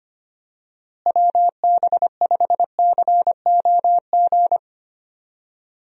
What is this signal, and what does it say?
Morse code, heard as a keyed tone, about 25 wpm, 700 Hz. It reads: WB5COG
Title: USA Call Signs - 6 Characters - Repeated 3x - 25wpm